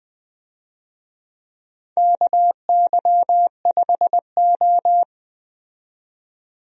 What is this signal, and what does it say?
Morse code, heard as a keyed tone, about 20 wpm, 700 Hz. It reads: KY5O
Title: USA Call Signs - 4 Characters - 20wpm